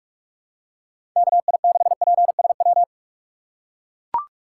Morse code, heard as a keyed tone, about 44 wpm, 700 Hz. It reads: KI6PSW
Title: USA Call Signs - 6 Characters - 6x Character Spacing - 45wpm